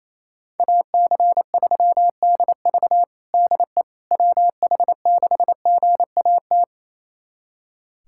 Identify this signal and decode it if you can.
Morse code, heard as a keyed tone, about 28 wpm, 700 Hz. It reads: AC3DV DE WH6GAT
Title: USA Callsign DE Callsign 28wpm